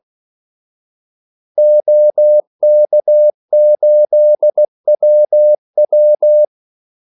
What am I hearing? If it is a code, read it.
Morse code, heard as a keyed tone, about 16 wpm, 600 Hz. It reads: OK8WW
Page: Kaldesignaler 16 wpm